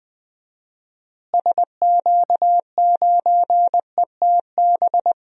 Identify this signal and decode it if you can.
Morse code, heard as a keyed tone, about 20 wpm, 700 Hz. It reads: SQ9ETB
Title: DXCC-100 Call Signs - 6 Characters - Repeated 2x - 20wpm